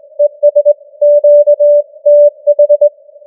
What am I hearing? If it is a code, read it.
Morse code, heard as a keyed tone, about 22 wpm, 590 Hz. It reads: ES QTH
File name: challenge-178